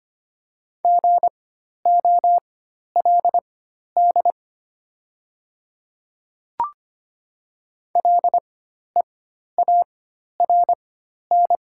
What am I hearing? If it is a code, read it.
Morse code, heard as a keyed tone, about 25 wpm, spaced as for 12 wpm, 690 Hz. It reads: GOLD LEARN
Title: 4x Character Spacing - Top 500 Words - 25wpm